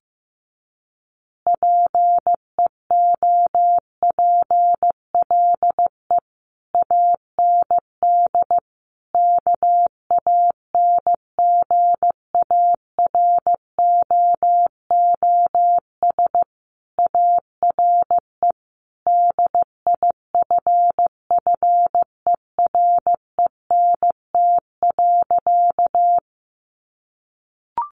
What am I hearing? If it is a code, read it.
Morse code, heard as a keyed tone, about 15 wpm, 700 Hz. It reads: PEOPLE AND KANGAROOS ARE DIFFERENT.
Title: Easy Phrases 15wpm